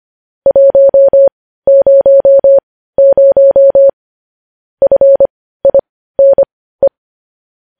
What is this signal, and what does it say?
Morse code, heard as a keyed tone, about 25 wpm, spaced as for 15 wpm, 550 Hz. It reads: 100 FINE
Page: news-headlines-in-morse-code-at-15-wpm